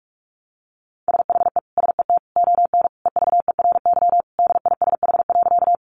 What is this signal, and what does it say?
Morse code, heard as a keyed tone, about 44 wpm, 700 Hz. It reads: SHE SET ON EVERY DISH.
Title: Sentences from Top 1300 Words 45wpm